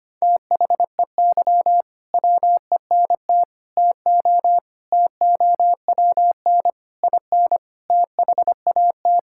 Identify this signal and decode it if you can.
Morse code, heard as a keyed tone, about 25 wpm, 700 Hz. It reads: THEY WENT TO TOWN IN THAT
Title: Sentences from Top 1000 Words - Review Part 5 - 25wpm